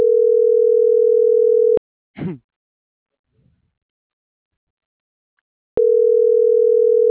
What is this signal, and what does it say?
Ringback tone (North America), 440 + 480 Hz.